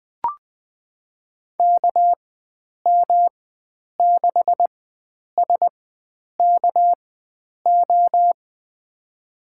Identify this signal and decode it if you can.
Morse code, heard as a keyed tone, about 20 wpm, spaced as for 9 wpm, 700 Hz. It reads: KM6SKO
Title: USA Call Signs - 6 Characters - 4x Character Spacing - 20wpm